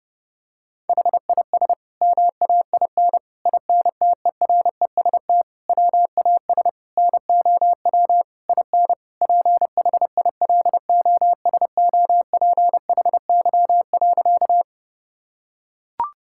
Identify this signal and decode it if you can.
Morse code, heard as a keyed tone, about 32 wpm, 700 Hz. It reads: HIS MAIN INTEREST WAS NOW IN PHILOSOPHY.
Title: Sentences from Top 1000 Words - Review Part 6 - 30wpm